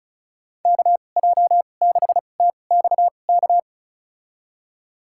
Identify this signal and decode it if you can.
Morse code, heard as a keyed tone, about 36 wpm, spaced as for 25 wpm, 700 Hz. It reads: KJ6TXK
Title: USA Call Signs - 6 Characters - 2x Character Spacing - 35wpm